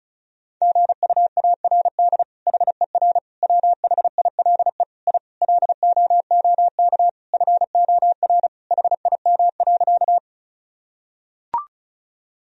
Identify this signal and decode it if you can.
Morse code, heard as a keyed tone, about 36 wpm, 700 Hz. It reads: GUARD HER WHILE I LOOK FOR HIM.
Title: Sentences from Top 1000 Words - Review Part 4 - 35wpm